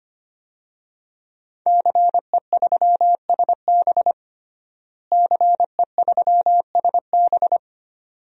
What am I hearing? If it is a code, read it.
Morse code, heard as a keyed tone, about 25 wpm, 700 Hz. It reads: CE3SB CE3SB
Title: DXCC-100 Call Signs - 5 Characters - Repeated 2x - 25wpm